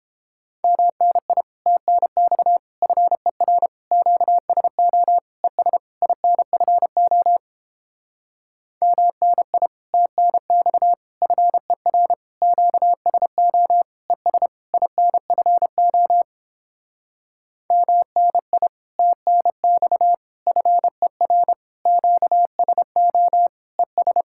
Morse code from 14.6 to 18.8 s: INFO MNI

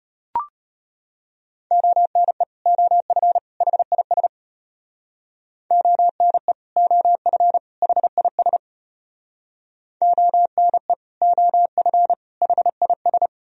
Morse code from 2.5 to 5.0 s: OF HIS

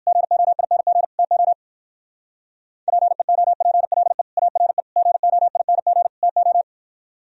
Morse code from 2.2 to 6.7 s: PEOPLE ARE GOING TO

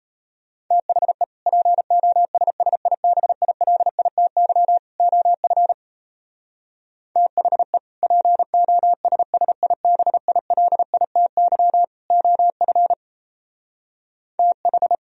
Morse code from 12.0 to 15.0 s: OF TH

